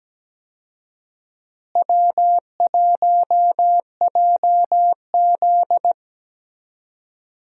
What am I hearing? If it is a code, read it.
Morse code, heard as a keyed tone, about 17 wpm, 700 Hz. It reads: W1JZ
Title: USA Call Signs - 4 Characters - Repeated 3x - 17wpm